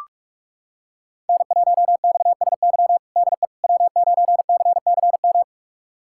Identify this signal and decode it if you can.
Morse code, heard as a keyed tone, about 42 wpm, 700 Hz. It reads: N1XSY DE W9CCM
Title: USA Callsign DE Callsign - 5 Characters - 45wpm